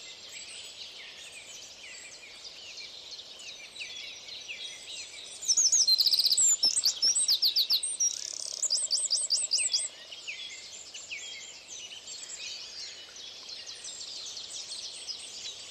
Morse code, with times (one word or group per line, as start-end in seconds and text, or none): none